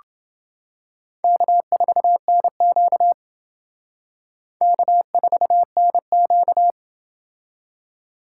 Morse code, heard as a keyed tone, about 28 wpm, 700 Hz. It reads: K4NQ K4NQ